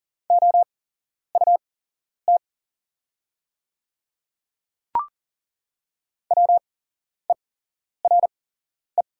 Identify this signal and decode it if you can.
Morse code, heard as a keyed tone, about 37 wpm, spaced as for 11 wpm, 690 Hz. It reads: OUT WERE
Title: 8x Character Spacing - Top 200 Words - 40wpm